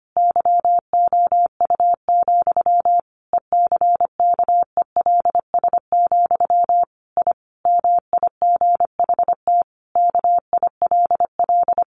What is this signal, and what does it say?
Morse code, heard as a keyed tone, about 25 wpm, 700 Hz. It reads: YOU, ECKELS, I MIGHT KILL
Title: A Sound Of Thunder 25wpm